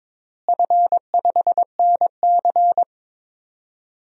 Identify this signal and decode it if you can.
Morse code, heard as a keyed tone, about 22 wpm, 700 Hz. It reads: F5NC
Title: DXCC-100 Call Signs - 4 Characters - Repeated 3x - 22wpm